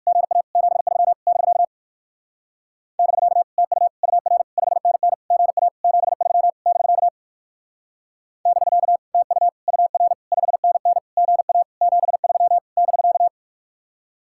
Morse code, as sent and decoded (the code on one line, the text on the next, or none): --. .- / --... ...-- / -...-.- / -...-.- / - ..- / ..- .-. / ..... -. -. / --. .- / --... ...-- / -...-.- / -...-.- / - ..- / ..- .-. / ..... -. -. / --. .- / --... ...-- / -...-.-
GA 73 <BK> <BK> TU UR 5NN GA 73 <BK> <BK> TU UR 5NN GA 73 <BK>